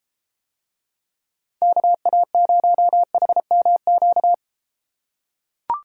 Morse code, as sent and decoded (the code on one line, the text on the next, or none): -.- .- ----- .... -- --.-
KA0HMQ